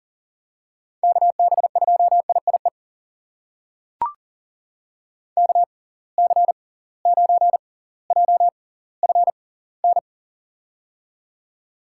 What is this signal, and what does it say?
Morse code, heard as a keyed tone, about 40 wpm, spaced as for 20 wpm, 700 Hz. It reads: KB2IIE KC9JFN
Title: USA Call Signs - 6 Characters - 6x Character Spacing - 40wpm